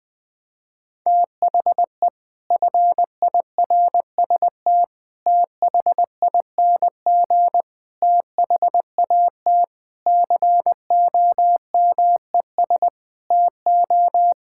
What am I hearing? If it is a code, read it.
Morse code, heard as a keyed tone, about 21 wpm, 700 Hz. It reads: THE FIRST THING THAT COMES TO
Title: Sets of 8 Words - 20wpm